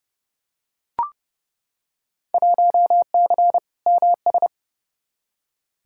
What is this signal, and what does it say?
Morse code, heard as a keyed tone, about 29 wpm, 700 Hz. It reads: 1C MS
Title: Field Day Exchanges Encore - 30wpm